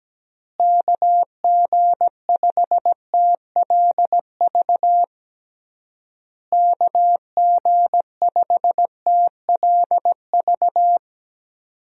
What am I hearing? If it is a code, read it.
Morse code, heard as a keyed tone, about 17 wpm, 700 Hz. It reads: KG5TLV KG5TLV
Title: USA Call Signs - 6 Characters - Repeated 3x - 17wpm